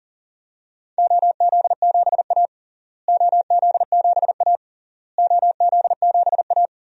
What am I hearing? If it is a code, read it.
Morse code, heard as a keyed tone, about 40 wpm, 700 Hz. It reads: OZ7A OZ7A OZ7A
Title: DXCC-100 Call Signs - 4 Characters - Repeated 3x - 40wpm